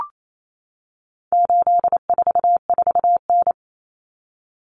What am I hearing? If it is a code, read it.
Morse code, heard as a keyed tone, about 28 wpm, 700 Hz. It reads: 844N
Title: Common Cut Numbers - 4-Digit 28wpm